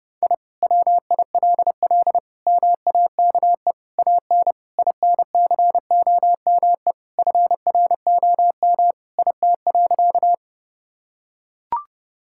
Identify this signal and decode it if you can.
Morse code, heard as a keyed tone, about 32 wpm, 700 Hz. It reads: I WILL MAKE AN INCOME FROM IT.
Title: Sentences from Top 1000 Words - Review Part 5 - 30wpm